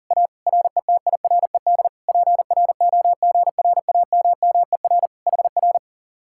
Morse code, heard as a keyed tone, about 48 wpm, 700 Hz. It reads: A RETIRED PROGRAMMER HR
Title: Speed-Racing Common QSO Phrases - 40wpm